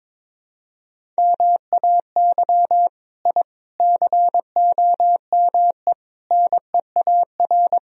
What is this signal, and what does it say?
Morse code, heard as a keyed tone, about 22 wpm, 700 Hz. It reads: MAY I COME NEAR
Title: Sentences from Top 1000 Words - Review Part 1 - 22wpm